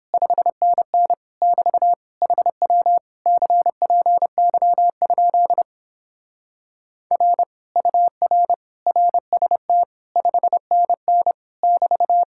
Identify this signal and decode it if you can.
Morse code, heard as a keyed tone, about 29 wpm, 700 Hz. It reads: 5NN <BT> HW CPY? R UR RST 5NN <BT>